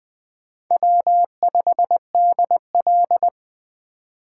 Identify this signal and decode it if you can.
Morse code, heard as a keyed tone, about 20 wpm, 700 Hz. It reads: W5DL
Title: Speed Racing - USA 1x2 Call Signs 17wpm